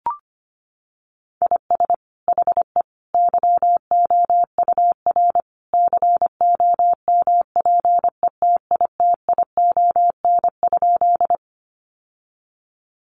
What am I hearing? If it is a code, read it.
Morse code, heard as a keyed tone, about 25 wpm, 700 Hz. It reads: IS HE YOUR COMPETITION?
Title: Sentences from Top 1000 Words 25wpm